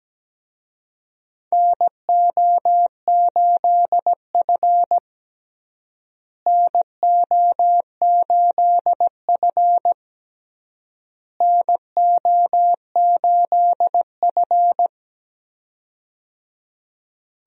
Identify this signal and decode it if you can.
Morse code, heard as a keyed tone, about 17 wpm, 700 Hz. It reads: NO8F NO8F NO8F
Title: USA Call Signs - 4 Characters - Repeated 3x - 17wpm